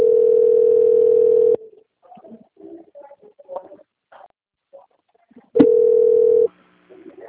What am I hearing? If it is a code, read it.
Ringback tone (North America), 440 + 480 Hz.
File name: rg-1600-unknown- - - .14248